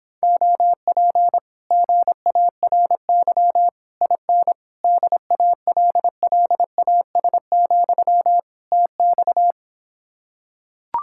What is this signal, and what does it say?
Morse code, heard as a keyed tone, about 26 wpm, 700 Hz. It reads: OP GARY IN DALLAS, TX